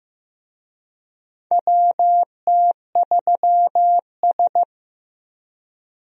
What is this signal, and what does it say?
Morse code, heard as a keyed tone, about 15 wpm, 700 Hz. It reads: WT3S
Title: USA Call Signs - 4 Characters - Repeated 3x - 15wpm